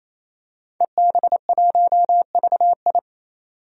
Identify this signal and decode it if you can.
Morse code, heard as a keyed tone, about 28 wpm, 690 Hz. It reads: EB1VI
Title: DXCC-100 Call Signs - 5 Characters - 28wpm